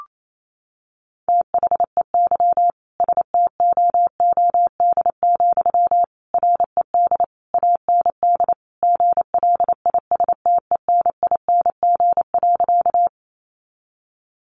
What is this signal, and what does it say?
Morse code, heard as a keyed tone, about 28 wpm, 700 Hz. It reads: THEY STOOD, RED AND GLISTENING.
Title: A Sound Of Thunder 28wpm